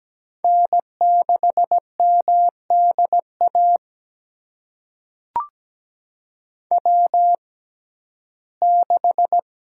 Morse code, heard as a keyed tone, about 17 wpm, spaced as for 12 wpm, 700 Hz. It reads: N6MDA W6
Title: USA Call Signs - 5 Characters - 6x Character Spacing - 17wpm